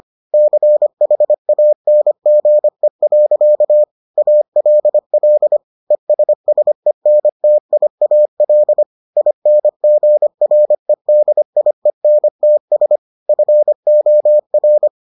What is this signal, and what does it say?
Morse code, heard as a keyed tone, about 25 wpm, 600 Hz. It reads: CHANGE. ALL ESSENTIAL INGREDIENTS FOR